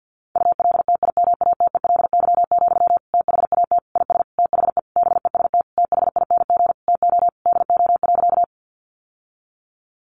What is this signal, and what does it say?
Morse code, heard as a keyed tone, about 49 wpm, 700 Hz. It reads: ULTIMATELY, THAT IS THE BEST THING TO DO.
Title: Sentences from Top 1200 Words 50wpm